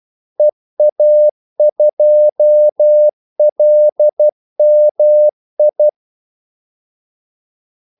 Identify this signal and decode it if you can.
Morse code, heard as a keyed tone, about 12 wpm, 600 Hz. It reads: EA2LMI
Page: 2 ekstra ordmellemrum mellem kaldesignalerne